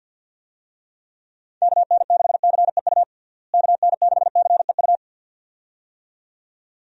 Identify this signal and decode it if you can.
Morse code, heard as a keyed tone, about 51 wpm, 700 Hz. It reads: KN6CEU KN6CEU
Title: USA Call Signs - 6 Characters - Repeated 2x - 50wpm